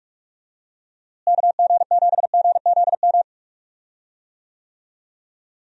Morse code, heard as a keyed tone, about 46 wpm, 700 Hz. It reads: KG7GZM
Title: USA Call Signs - 6 Characters - Repeated 2x - 45wpm